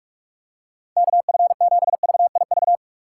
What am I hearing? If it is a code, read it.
Morse code, heard as a keyed tone, about 46 wpm, 700 Hz. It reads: KF7VIV
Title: USA Call Signs - 6 Characters - 2x Character Spacing - 45wpm